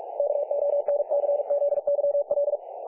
Morse code, heard as a keyed tone, about 44 wpm, 570 Hz. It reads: KONCZYL
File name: dlazaawansowanych / nagraniazpasma / websdr recording 3564.7kHz